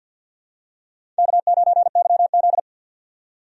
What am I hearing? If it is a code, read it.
Morse code, heard as a keyed tone, about 51 wpm, 700 Hz. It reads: K9YZ
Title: USA Call Signs - 4 Characters - 50wpm